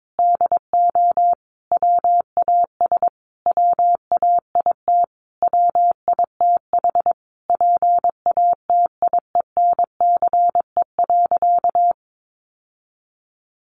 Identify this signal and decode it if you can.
Morse code, heard as a keyed tone, about 22 wpm, 700 Hz. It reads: DO WAS WAIT WITH PATIENCE.
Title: Sentences from Top 1300 Words 22wpm